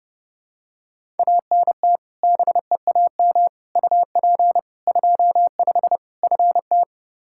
Morse code, heard as a keyed tone, about 30 wpm, 700 Hz. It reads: ANT BEAM UP 25 FT
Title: K1USN SST QSO Practice 30wpm